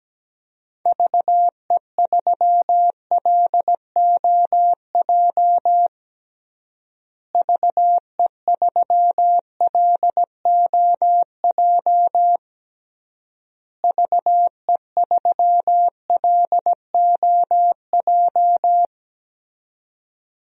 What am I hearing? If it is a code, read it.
Morse code, heard as a keyed tone, about 17 wpm, 700 Hz. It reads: VE3LOJ VE3LOJ VE3LOJ
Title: DXCC-100 Call Signs - 6 Characters - Repeated 3x - 17wpm